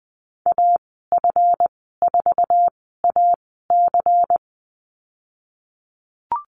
Morse code, heard as a keyed tone, about 20 wpm, spaced as for 15 wpm, 700 Hz. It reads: AF4AC
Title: 173 Lesson 28g - Number 9 - Call Signs 20wpm